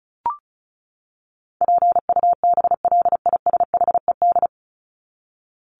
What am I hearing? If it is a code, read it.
Morse code, heard as a keyed tone, about 34 wpm, 700 Hz. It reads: PUBLISHED
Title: ED Suffix - Top 100 Words with Base Words 35wpm